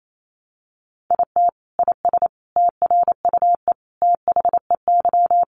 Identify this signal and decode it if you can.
Morse code, heard as a keyed tone, about 28 wpm, 700 Hz. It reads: IT IS TRUE THEY
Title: Sentences from Top 1400 Words 28wpm